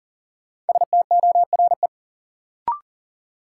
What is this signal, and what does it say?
Morse code, heard as a keyed tone, about 40 wpm, 700 Hz. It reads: STORE
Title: Sentences from Top 1000 Words - Review Part 5 - New Word Drill - 40wpm